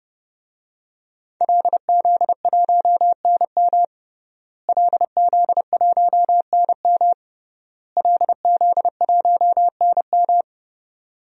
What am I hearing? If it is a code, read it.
Morse code, heard as a keyed tone, about 30 wpm, 700 Hz. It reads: LZ1NM LZ1NM LZ1NM
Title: DXCC-100 Call Signs - 5 Characters - Repeated 3x - 30wpm